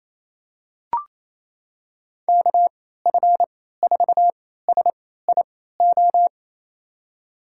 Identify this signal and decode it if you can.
Morse code, heard as a keyed tone, about 28 wpm, spaced as for 16 wpm, 700 Hz. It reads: KF4SIO
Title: USA Call Signs - 6 Characters - 3x Character Spacing - 28wpm